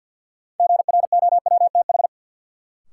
Morse code, heard as a keyed tone, about 53 wpm, 700 Hz. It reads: GROWTH
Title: Sentences from Top 1300 Words - New Word Drill Alt - 50wpm